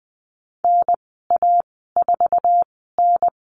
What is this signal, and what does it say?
Morse code, heard as a keyed tone, about 20 wpm, spaced as for 14 wpm, 700 Hz. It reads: NA4N
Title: 055 Lesson 10g - Letter r - Call Signs 20wpm